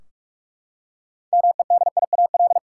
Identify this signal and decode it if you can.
Morse code, heard as a keyed tone, about 46 wpm, 700 Hz. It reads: MEDIAL
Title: Practice QSO - Exchange 3 - More Info - 45wpm